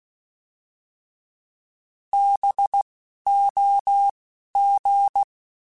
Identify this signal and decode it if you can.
Morse code, heard as a keyed tone, about 16 wpm, spaced as for 11 wpm, 800 Hz. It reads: BOG
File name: Alphabetical / Alphabetical mp3s / Bs @10